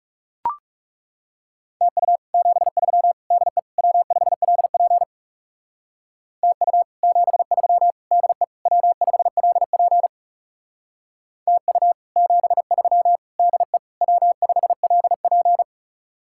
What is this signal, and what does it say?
Morse code, heard as a keyed tone, about 40 wpm, 700 Hz. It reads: TU 73 DE W5LP TU 73 DE W5LP TU 73 DE W5LP